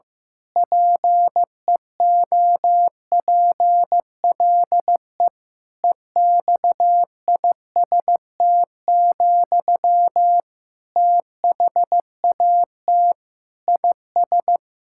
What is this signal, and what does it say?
Morse code, heard as a keyed tone, about 15 wpm, 700 Hz. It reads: PEOPLE EXIST, THAT IS
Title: Sentences from Top 1000 Words - Review Part 5 - 15wpm